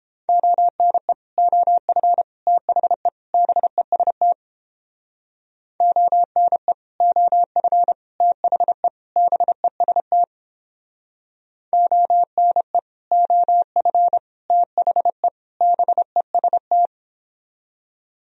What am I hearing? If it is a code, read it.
Morse code, heard as a keyed tone, about 29 wpm, 700 Hz. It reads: ONE OF THE BEST ONE OF THE BEST ONE OF THE BEST